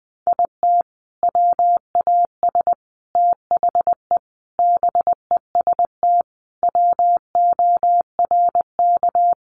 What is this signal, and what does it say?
Morse code, heard as a keyed tone, about 20 wpm, 700 Hz. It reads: IT WAS THE BEST WORK
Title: Sentences from Top 1300 Words 20wpm